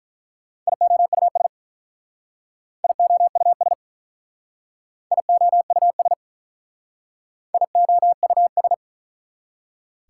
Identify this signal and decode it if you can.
Morse code, heard as a keyed tone, about 44 wpm, 690 Hz. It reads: IOUS IOUS IOUS IOUS